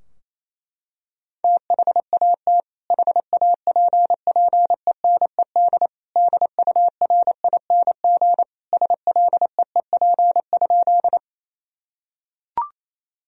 Morse code, heard as a keyed tone, about 27 wpm, 700 Hz. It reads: THAT HAPPENED DURING SLEEP?
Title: Sentences from Top 1000 Words - Review Part 2 - 28wpm